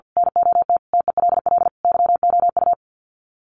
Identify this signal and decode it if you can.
Morse code, heard as a keyed tone, about 50 wpm, 700 Hz. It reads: NOT TELL YOU
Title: Sets of 5 Words 50wpm